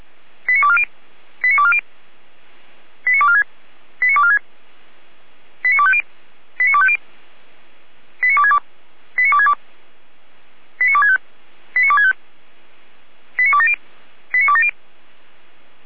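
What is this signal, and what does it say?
Five-tone selective call (ZVEI), heard as quick runs of five tones, tones of about 70 ms. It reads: 89240 89240 89246 89246 89250 89250 89262 89262 89265 89265 89270 89270
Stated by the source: ZVEI five-tone call received by SDR, recorded for Feuerwehr Orlamünde.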